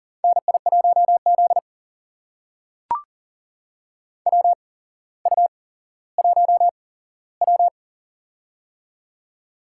Morse code, heard as a keyed tone, about 40 wpm, spaced as for 17 wpm, 700 Hz. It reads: NI1Z WU1W
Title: USA Call Signs - 4 Characters - 8x Character Spacing - 40wpm